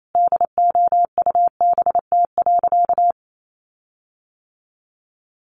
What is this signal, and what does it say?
Morse code, heard as a keyed tone, about 28 wpm, 700 Hz. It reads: DOUBT.
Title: Sentences from Top 1500 Words 28wpm